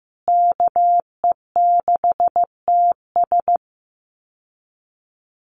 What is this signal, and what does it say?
Morse code, heard as a keyed tone, about 15 wpm, 700 Hz. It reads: KE6TS
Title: USA Call Signs Repeated 3x 15wpm